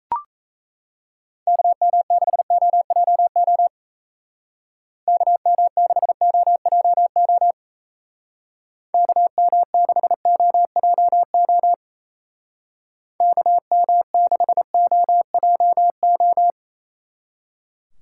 Morse code, heard as a keyed tone, about 35 wpm, 700 Hz. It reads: KM6OJO KM6OJO KM6OJO KM6OJO